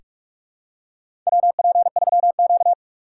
Morse code, heard as a keyed tone, about 46 wpm, 700 Hz. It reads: WP3Q
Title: USA Call Signs - 4 Characters - 2x Character Spacing - 45wpm